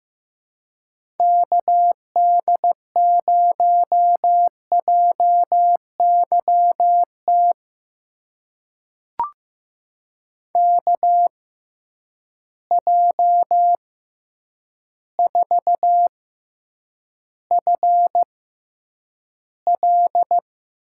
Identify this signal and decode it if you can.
Morse code, heard as a keyed tone, about 15 wpm, spaced as for 8 wpm, 700 Hz. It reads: KD0JYT KJ4FL
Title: USA Call Signs - 6 Characters - 6x Character Spacing - 15wpm